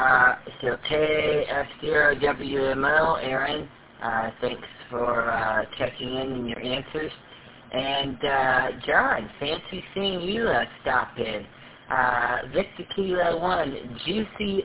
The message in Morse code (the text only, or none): none